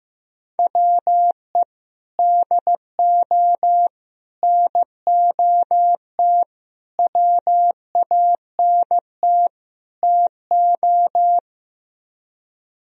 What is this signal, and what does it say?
Morse code, heard as a keyed tone, about 15 wpm, 700 Hz. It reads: WE DO NOT WANT TO
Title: Speed-Racing - Sets of 5 Words 15wpm